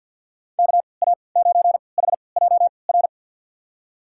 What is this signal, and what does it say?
Morse code, heard as a keyed tone, about 53 wpm, spaced as for 29 wpm, 700 Hz. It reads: KA9HJR
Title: USA Call Signs - 6 Characters - 3x Character Spacing - 50wpm